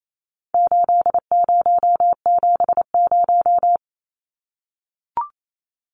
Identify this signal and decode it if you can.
Morse code, heard as a keyed tone, about 28 wpm, 700 Hz. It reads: 8070
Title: Numbers - 4-Digit 28wpm